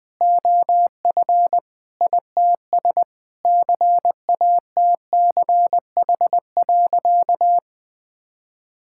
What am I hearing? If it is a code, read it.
Morse code, heard as a keyed tone, about 20 wpm, 700 Hz. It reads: OF ITS CATCH.
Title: Sentences with 2-Word Phrases - Spoken Beginning - 20wpm